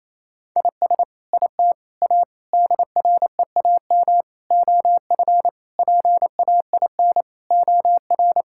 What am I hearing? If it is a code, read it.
Morse code, heard as a keyed tone, about 28 wpm, 700 Hz. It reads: IS IT A DREAM OF PAIN OR
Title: Sentences from Top 1600 Words 28wpm